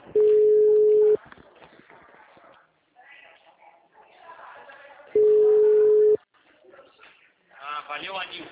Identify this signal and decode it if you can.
Ringback tone (Europe, 425 Hz), with speech.